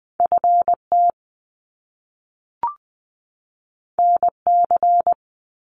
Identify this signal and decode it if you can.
Morse code, heard as a keyed tone, about 20 wpm, 700 Hz. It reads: FT NC
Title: Top Two Letter Combinations in Words 20wpm